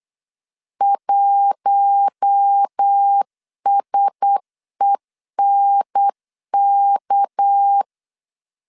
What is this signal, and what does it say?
Morse code, heard as a keyed tone, about 8 wpm, 800 Hz. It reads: 1SENK
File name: LEZIONI / 6 PKX0 / 6 PK 8wpm 3003